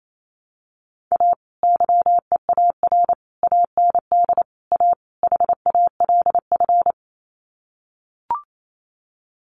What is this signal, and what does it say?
Morse code, heard as a keyed tone, about 28 wpm, 700 Hz. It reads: A YEAR AND A HALF
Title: Sets of 5 Words 28wpm